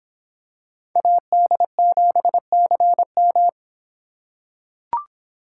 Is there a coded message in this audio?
Yes: Morse code from 1.0 to 3.5 s, heard as a keyed tone, about 25 wpm, 700 Hz. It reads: AD7CM